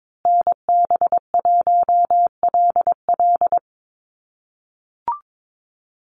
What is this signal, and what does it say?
Morse code, heard as a keyed tone, about 22 wpm, 700 Hz. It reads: NB1LL
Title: USA Call Signs Repeated 3x 22wpm